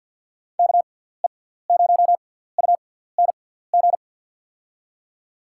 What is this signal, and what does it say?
Morse code, heard as a keyed tone, about 49 wpm, spaced as for 17 wpm, 700 Hz. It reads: KE0UNG
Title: USA Call Signs - 6 Characters - 6x Character Spacing - 50wpm